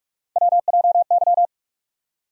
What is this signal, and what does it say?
Morse code, heard as a keyed tone, about 45 wpm, 700 Hz. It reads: WJY
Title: USA Call Sign Postfixes - 3 Letters 45wpm